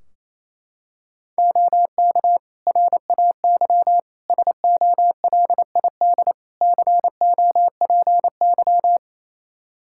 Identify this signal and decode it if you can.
Morse code, heard as a keyed tone, about 28 wpm, 700 Hz. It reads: OK RAY SOLID COPY
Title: Practice QSO - Exchange 3 - More Info - 28wpm